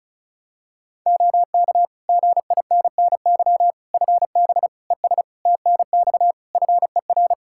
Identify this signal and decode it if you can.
Morse code, heard as a keyed tone, about 36 wpm, 700 Hz. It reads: OK GINNY FB ES TNX FER
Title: Practice QSO - Exchange 2 - Info - 35wpm